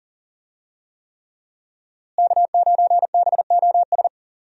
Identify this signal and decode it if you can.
Morse code, heard as a keyed tone, about 40 wpm, 690 Hz. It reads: K9BOS
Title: USA Call Signs - 5 Characters - Repeated 3x - 40wpm